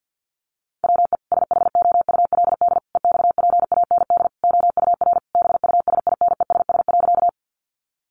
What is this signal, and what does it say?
Morse code, heard as a keyed tone, about 50 wpm, 700 Hz. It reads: WE SHOULD EXPAND OUR BUSINESS.
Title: Sentences from Top 1400 Words 50wpm